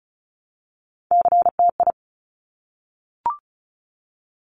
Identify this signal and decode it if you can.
Morse code, heard as a keyed tone, about 36 wpm, 700 Hz. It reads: CTI
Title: Top Three Letter Combinations in Words 35wpm